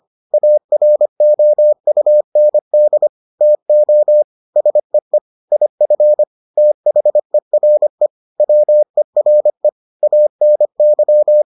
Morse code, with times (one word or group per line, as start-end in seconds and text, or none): 0.3-3.1 AROUND
3.4-4.2 TO
4.6-5.2 SEE
5.5-6.2 IF
6.6-8.1 THERE
8.4-9.7 WERE
10.0-11.4 ANY